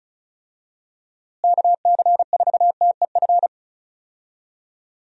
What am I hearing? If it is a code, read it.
Morse code, heard as a keyed tone, about 36 wpm, 700 Hz. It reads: KC4TEF
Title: USA Call Signs - 6 Characters - Repeated 2x - 35wpm